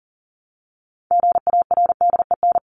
Morse code, heard as a keyed tone, about 39 wpm, 700 Hz. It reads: GARDEN
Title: Top 501-750 Words 40wpm